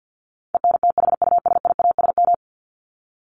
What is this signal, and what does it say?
Morse code, heard as a keyed tone, about 49 wpm, 700 Hz. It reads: ENTHUSIASM
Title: Top 1751-2000 Words 50wpm